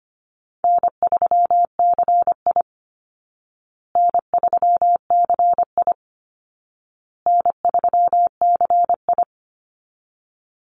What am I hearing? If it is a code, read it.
Morse code, heard as a keyed tone, about 25 wpm, 700 Hz. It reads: N3CI N3CI N3CI
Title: Easy Call Signs Repeated 3x - 25wpm